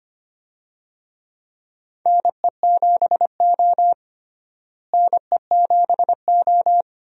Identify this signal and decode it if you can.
Morse code, heard as a keyed tone, about 25 wpm, 700 Hz. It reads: NE7O NE7O
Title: USA Call Signs - 4 Characters - Repeated 2x - 25wpm